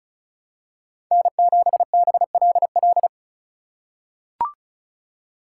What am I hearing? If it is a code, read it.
Morse code, heard as a keyed tone, about 36 wpm, 700 Hz. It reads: N7BLL
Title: USA Call Signs - 5 Characters - 8x Character Spacing - 35wpm